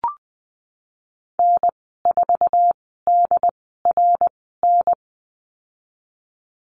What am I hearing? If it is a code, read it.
Morse code, heard as a keyed tone, about 20 wpm, spaced as for 14 wpm, 700 Hz. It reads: N4DRN
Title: 088 Lesson 15g - Number 5 - Call Signs 20wpm